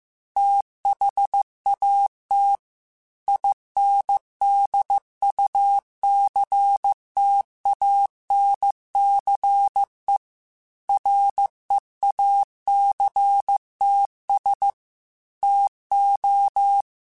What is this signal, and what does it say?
Morse code, heard as a keyed tone, about 15 wpm, 800 Hz. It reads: THAT INDUCTANCE REACTS TO
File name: Technical Subjects / Technical Subjects-mp3s / Reactance @13